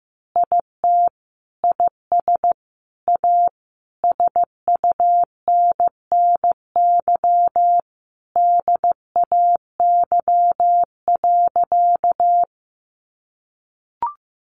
Morse code, heard as a keyed tone, about 15 wpm, 700 Hz. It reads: IT IS A SUNNY DAY.
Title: Easy Phrases 15wpm